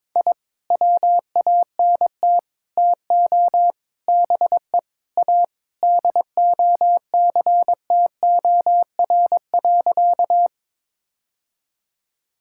Morse code, heard as a keyed tone, about 21 wpm, 700 Hz. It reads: I WANT TO BE A DOCTOR.
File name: practice / 22wpm / Sentences with 3-Word Phrases - 22wpm